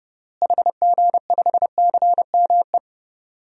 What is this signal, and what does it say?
Morse code, heard as a keyed tone, about 32 wpm, 700 Hz. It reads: HG5CME
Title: DXCC-100 Call Signs - 6 Characters - Repeated 3x - 30wpm